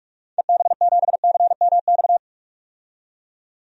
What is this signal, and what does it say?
Morse code, heard as a keyed tone, about 46 wpm, 700 Hz. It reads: EB7CMX
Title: DXCC-100 Call Signs - 6 Characters - 45wpm